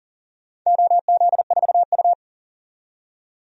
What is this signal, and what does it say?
Morse code, heard as a keyed tone, about 40 wpm, 700 Hz. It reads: OZ4U
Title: DXCC-100 Call Signs - 4 Characters - 40wpm